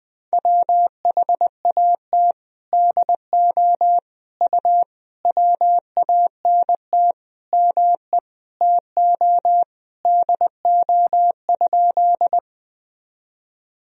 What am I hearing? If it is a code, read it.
Morse code, heard as a keyed tone, about 20 wpm, 700 Hz. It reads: WHAT DO U WANT ME TO DO?
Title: Sentences from Top 200 Words and Abbreviations 20wpm